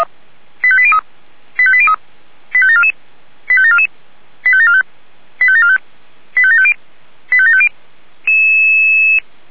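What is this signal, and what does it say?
Five-tone selective call (ZVEI), heard as quick runs of five tones, tones of about 70 ms. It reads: 85803 85803 85744 85744 85745 85745 85750 85750
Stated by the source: Five-tone ZVEI selective call received by SDR.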